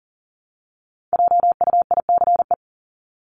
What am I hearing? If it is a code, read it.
Morse code, heard as a keyed tone, about 40 wpm, 700 Hz. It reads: JUICE
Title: Top 1251-1500 Words 40wpm